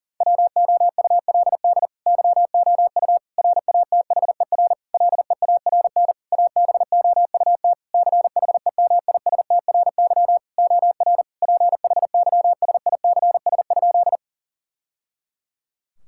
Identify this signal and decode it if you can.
Morse code, heard as a keyed tone, about 40 wpm, 700 Hz. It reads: WOULD YOU RATHER LEARN ABOUT CHEMISTRY OR PHYSICS?
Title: Sentences from Top 1000 Words - Review Part 6 - 40wpm